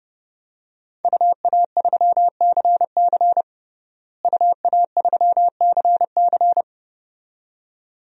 Morse code, heard as a keyed tone, about 32 wpm, 700 Hz. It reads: UA3CC UA3CC
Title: DXCC-100 Call Signs - 5 Characters - Repeated 2x - 30wpm